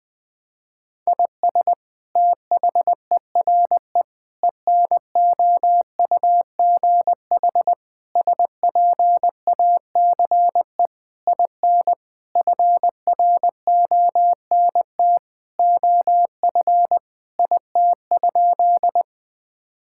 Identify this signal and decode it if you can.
Morse code, heard as a keyed tone, about 20 wpm, 700 Hz. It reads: IS THERE ENOUGH SPACE IN FRONT OF IT?
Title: Sentences from Top 1000 Words - Review Part 5 - 20wpm